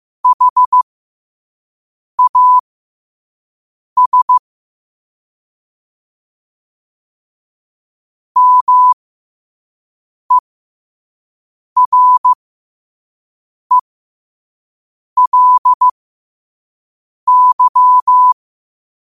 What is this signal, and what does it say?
Morse code, heard as a keyed tone, about 15 wpm, spaced as for 5 wpm, 1000 Hz. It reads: HAS MERELY
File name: qotdfiles / 5wpm